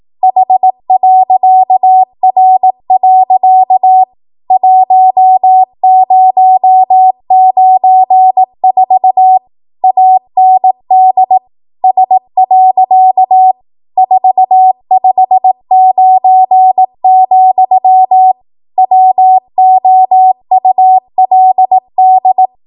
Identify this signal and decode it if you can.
Morse code, heard as a keyed tone, about 18 wpm, 750 Hz. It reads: H.R. 1094 AND S. 459, WOULD
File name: Morse / 18WPM